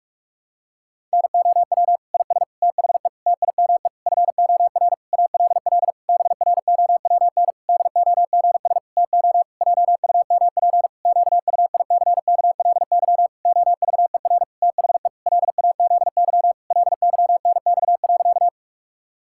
Morse code, heard as a keyed tone, about 46 wpm, 700 Hz. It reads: NOW IS THE TIME FOR ALL BROWN DOGS TO JUMP QUICKLY OVER THE LAZY LYNX.
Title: Pangram Challenge v2 - 45wpm